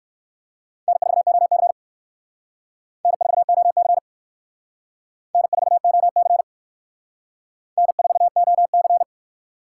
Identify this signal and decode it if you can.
Morse code, heard as a keyed tone, about 58 wpm, 700 Hz. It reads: N4OC N4OC N4OC N4OC